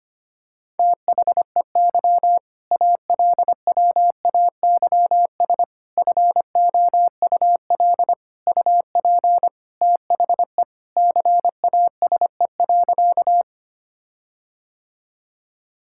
Morse code, heard as a keyed tone, about 25 wpm, 700 Hz. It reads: THEY ALWAYS FOUL UP THE CASE.
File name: practice / 25wpm / All-Age Appropriate Jokes - 25wpm